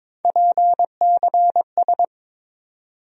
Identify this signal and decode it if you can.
Morse code, heard as a keyed tone, about 22 wpm, 700 Hz. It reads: PCS
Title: USA Call Sign Postfixes - 3 Letters 22wpm